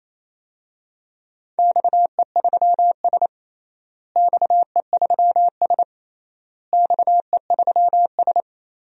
Morse code, heard as a keyed tone, about 27 wpm, 700 Hz. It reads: XE3S XE3S XE3S
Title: DXCC-100 Call Signs - 4 Characters - Repeated 3x - 28wpm